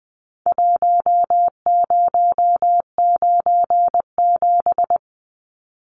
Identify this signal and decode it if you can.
Morse code, heard as a keyed tone, about 20 wpm, 700 Hz. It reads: 1097
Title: Numbers - Contest Style - 4-Digit 20wpm